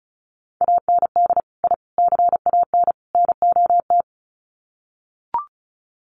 Morse code, heard as a keyed tone, about 34 wpm, 700 Hz. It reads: AND I CAN NOT
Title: Sets of 4 Words 35wpm